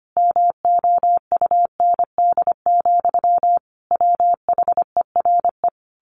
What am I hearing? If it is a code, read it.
Morse code, heard as a keyed tone, about 25 wpm, 700 Hz. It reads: MOUND, WHERE
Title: A Sound Of Thunder 25wpm